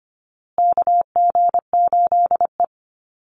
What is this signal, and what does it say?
Morse code, heard as a keyed tone, about 25 wpm, 700 Hz. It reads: KG8E
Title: USA Call Signs Repeated 3x 25wpm